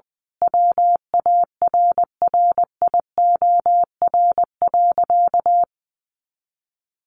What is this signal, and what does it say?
Morse code, heard as a keyed tone, about 20 wpm, 700 Hz. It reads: WARRIOR.
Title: A Sound Of Thunder 20wpm